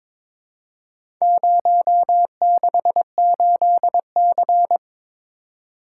Morse code, heard as a keyed tone, about 23 wpm, 700 Hz. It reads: 068C
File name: practice / 22wpm / Four Letter-Number Combinations - 22wpm